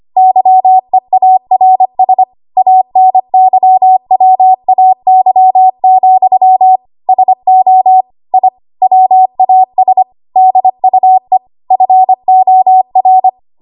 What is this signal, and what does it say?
Morse code, heard as a keyed tone, about 25 wpm, 750 Hz. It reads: YEARS ANYWAY, SO I WAS DUE FOR
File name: Archive / 25 WPM / 25WPM